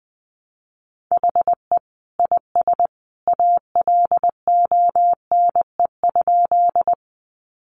Morse code, heard as a keyed tone, about 20 wpm, 700 Hz. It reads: HE IS ALONE?
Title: Sentences from Top 1000 Words 20wpm